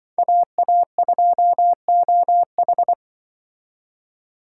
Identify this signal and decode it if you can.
Morse code, heard as a keyed tone, about 23 wpm, 700 Hz. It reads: AA2OH
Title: Speed Racing - USA 2x2 Call Signs 20wpm